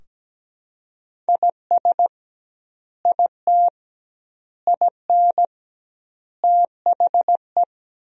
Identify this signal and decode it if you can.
Morse code, heard as a keyed tone, about 17 wpm, 700 Hz. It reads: IS IT IN THE
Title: 2x Word-Spacing - Sentences from Top 300 Words - 17wpm